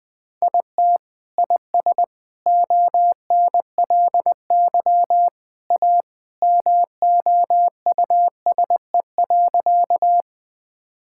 Morse code, heard as a keyed tone, about 20 wpm, 700 Hz. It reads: IT IS ONLY A MOUSE.
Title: Sentences from Top 1000 Words - Review Part 5 - 20wpm